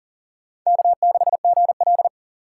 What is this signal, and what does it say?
Morse code, heard as a keyed tone, about 51 wpm, 700 Hz. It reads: K6GL